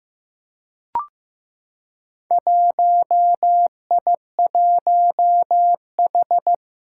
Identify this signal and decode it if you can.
Morse code, heard as a keyed tone, about 15 wpm, 700 Hz. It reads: 1I1H
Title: Four Challenging Letter-Number Combinations 15wpm